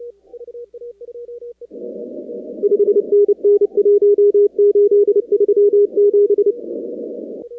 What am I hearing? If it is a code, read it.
Morse code, heard as a keyed tone, about 30 wpm, 410 Hz. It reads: 5NN 1837
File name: contests / 2019-cw-wpx-kt5j-k5pi- -2330 0-45